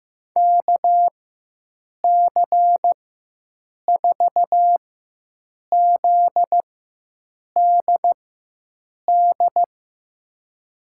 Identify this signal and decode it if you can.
Morse code, heard as a keyed tone, about 15 wpm, spaced as for 7 wpm, 700 Hz. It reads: KC4ZDD
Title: USA Call Signs - 6 Characters - 4x Character Spacing - 15wpm